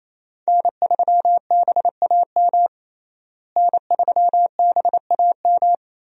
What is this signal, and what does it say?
Morse code, heard as a keyed tone, about 28 wpm, 700 Hz. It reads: N3BAM N3BAM
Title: USA Call Signs - 5 Characters - Repeated 2x - 28wpm